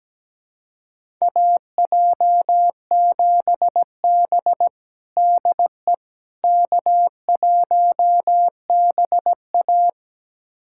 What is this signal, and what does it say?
Morse code, heard as a keyed tone, about 17 wpm, 700 Hz. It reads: AJ7B DE K1BA
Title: USA Callsign DE Callsign - 4 Characters - 17wpm